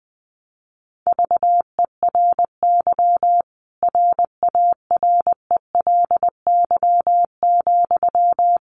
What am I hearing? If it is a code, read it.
Morse code, heard as a keyed tone, about 20 wpm, 700 Hz. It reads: VERY RARELY,
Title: Sentences from Top 1400 Words 20wpm